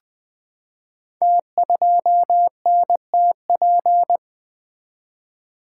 Morse code, heard as a keyed tone, about 20 wpm, 700 Hz. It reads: T2NTP
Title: Code Groups - 5 Letter:Numbers - 20wpm